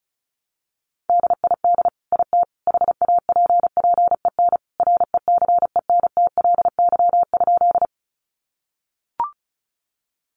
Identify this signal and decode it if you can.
Morse code, heard as a keyed tone, about 34 wpm, 700 Hz. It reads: DID IT HAPPEN RECENTLY?
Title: Sentences from Top 1200 Words 35wpm